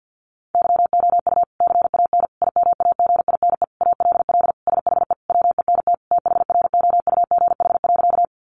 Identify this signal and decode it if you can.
Morse code, heard as a keyed tone, about 50 wpm, 700 Hz. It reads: YOU CAN IMAGINE ALL SHE WENT THROUGH.
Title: Sentences from Top 1400 Words 50wpm